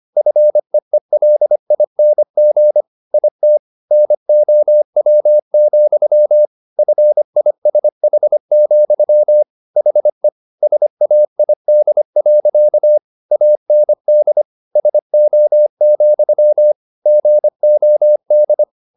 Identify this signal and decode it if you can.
Morse code, heard as a keyed tone, about 25 wpm, 600 Hz. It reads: FEELING IT NOW, FISH, HE SAID. AND SO, GOD
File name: the old man and the sea part 1 25wpm